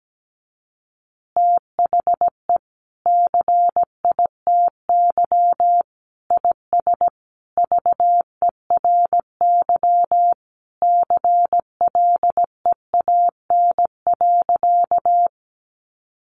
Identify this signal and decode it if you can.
Morse code, heard as a keyed tone, about 17 wpm, 700 Hz. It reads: THE CITY IS VERY CLEAN.
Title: Easy Phrases 17wpm